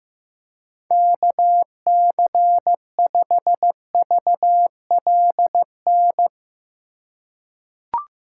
Morse code, heard as a keyed tone, about 15 wpm, 700 Hz. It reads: KC5VLN
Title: USA Call Signs - 6 Characters - 3x Character Spacing - 15wpm